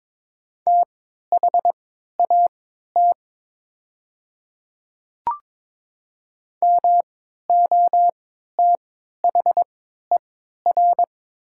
Morse code, heard as a keyed tone, about 22 wpm, spaced as for 13 wpm, 700 Hz. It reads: THAT MOTHER
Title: 3x Character Spacing - Top 300 Words - 22wpm